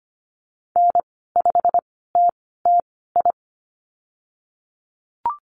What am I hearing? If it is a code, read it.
Morse code, heard as a keyed tone, about 25 wpm, spaced as for 16 wpm, 700 Hz. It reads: N5TTI
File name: Lesson 15g - Number 5 - Call Signs 25wpm